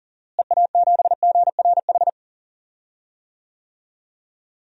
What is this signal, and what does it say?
Morse code, heard as a keyed tone, about 40 wpm, 700 Hz. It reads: EA7GRH
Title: DXCC-100 Call Signs - 6 Characters - Repeated 2x - 40wpm